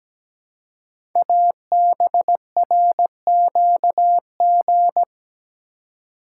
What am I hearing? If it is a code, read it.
Morse code, heard as a keyed tone, about 17 wpm, 700 Hz. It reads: ABRQG
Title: Code Groups - 5 Letters - 17wpm